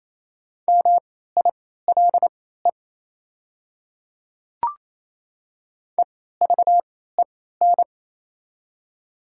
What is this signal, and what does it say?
Morse code, heard as a keyed tone, about 28 wpm, spaced as for 16 wpm, 690 Hz. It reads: MILE EVEN
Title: 3x Character Spacing - Top 300 Words - 28wpm